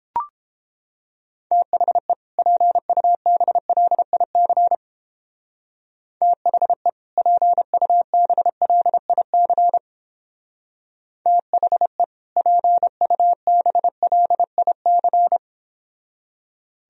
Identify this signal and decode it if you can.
Morse code, heard as a keyed tone, about 29 wpm, 700 Hz. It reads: THE PUBLIC THE PUBLIC THE PUBLIC